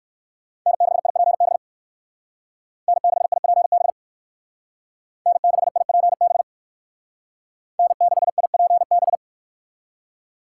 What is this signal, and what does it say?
Morse code, heard as a keyed tone, about 58 wpm, 700 Hz. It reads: N6IPB N6IPB N6IPB N6IPB